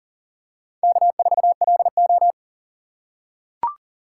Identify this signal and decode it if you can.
Morse code, heard as a keyed tone, about 43 wpm, 690 Hz. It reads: K4LO
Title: USA Call Signs - 4 Characters - 6x Character Spacing - 40wpm